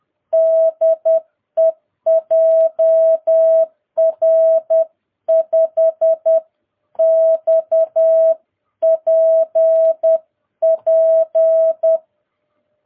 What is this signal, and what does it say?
Morse code, heard as a keyed tone, about 10 wpm, 650 Hz. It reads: DE JR5XPP
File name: jh8cbh / 2022cw